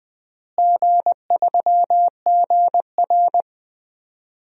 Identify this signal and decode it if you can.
Morse code, heard as a keyed tone, about 20 wpm, 700 Hz. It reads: G3GR
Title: DXCC-100 Call Signs - 4 Characters - Repeated 2x - 20wpm